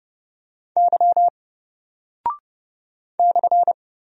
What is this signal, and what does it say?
Morse code, heard as a keyed tone, about 30 wpm, 700 Hz. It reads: Y /
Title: Single Character (Letter:Number:Punctuation:Prosign) Alt 30wpm